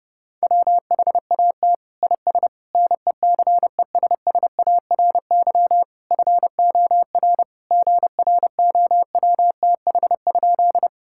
Morse code, heard as a keyed tone, about 32 wpm, 700 Hz. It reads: WHAT IS NECESSARY FOR GROWTH?
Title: Sentences from Top 1000 Words - Review Part 5 - 30wpm